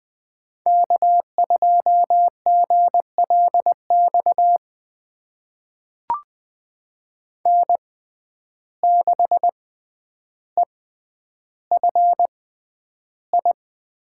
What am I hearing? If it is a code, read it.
Morse code, heard as a keyed tone, about 20 wpm, spaced as for 10 wpm, 700 Hz. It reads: K2GLX N6EFI
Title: USA Call Signs - 5 Characters - 6x Character Spacing - 20wpm